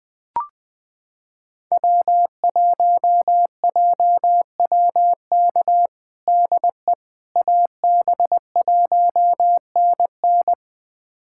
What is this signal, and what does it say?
Morse code, heard as a keyed tone, about 20 wpm, 700 Hz. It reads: W1JWK DE AB1NN
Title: USA Callsign DE Callsign - 5 Characters - 20wpm